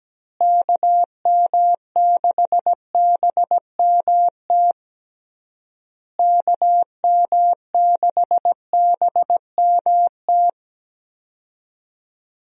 Morse code, heard as a keyed tone, about 17 wpm, 700 Hz. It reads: KM6BMT KM6BMT
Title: USA Call Signs - 6 Characters - Repeated 3x - 17wpm